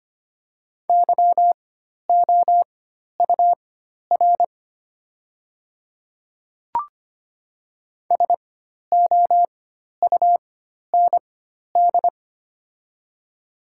Morse code, heard as a keyed tone, about 25 wpm, spaced as for 12 wpm, 690 Hz. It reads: YOUR SOUND
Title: 4x Character Spacing - Top 100 Words - 25wpm